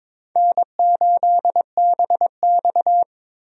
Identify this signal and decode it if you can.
Morse code, heard as a keyed tone, about 22 wpm, 700 Hz. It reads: N8BX
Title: USA Call Signs - 4 Characters - Repeated 3x - 22wpm